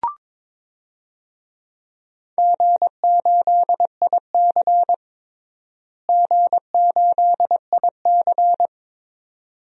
Morse code, heard as a keyed tone, about 22 wpm, 700 Hz. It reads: G8IC G8IC
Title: DXCC-100 Call Signs - 4 Characters - Repeated 2x - 22wpm